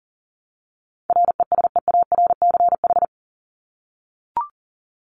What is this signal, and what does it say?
Morse code, heard as a keyed tone, about 40 wpm, 700 Hz. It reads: RESEARCH
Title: Top 501-750 Words 40wpm